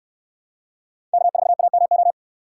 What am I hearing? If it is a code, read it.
Morse code, heard as a keyed tone, about 67 wpm, 700 Hz. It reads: K4AMO